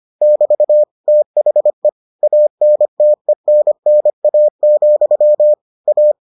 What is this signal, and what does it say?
Morse code, heard as a keyed tone, about 25 wpm, 600 Hz. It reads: <BT> THE ANTENNA, A